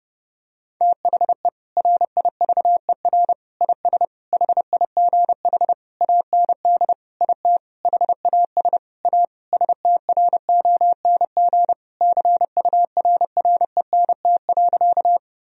Morse code, heard as a keyed tone, about 32 wpm, 700 Hz. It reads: THE RIVER IS HIGH AND IT HAS A STRONG CURRENT.
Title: Sentences from Top 1000 Words - Review Part 4 - 30wpm